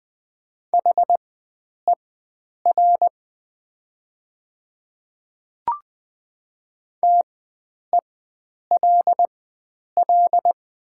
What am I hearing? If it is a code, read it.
Morse code, heard as a keyed tone, about 20 wpm, spaced as for 9 wpm, 700 Hz. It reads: HER TELL
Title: 4x Character Spacing - Top 500 Words - 20wpm